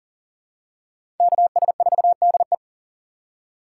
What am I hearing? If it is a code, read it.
Morse code, heard as a keyed tone, about 40 wpm, 700 Hz. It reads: KS4DE
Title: USA Call Signs - 4x Character Spacing - 40wpm